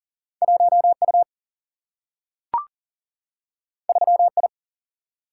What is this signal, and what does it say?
Morse code, heard as a keyed tone, about 40 wpm, 700 Hz. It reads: 1 U 3 I
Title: Single-Letter Single-Number 40wpm